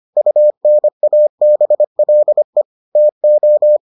Morse code, heard as a keyed tone, about 25 wpm, 600 Hz. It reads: UNABLE TO